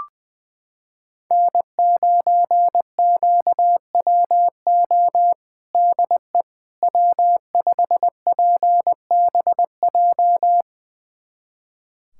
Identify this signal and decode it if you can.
Morse code, heard as a keyed tone, about 21 wpm, 700 Hz. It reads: N9QWO DE W5PBJ
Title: USA Callsign DE Callsign - 5 Characters - 20wpm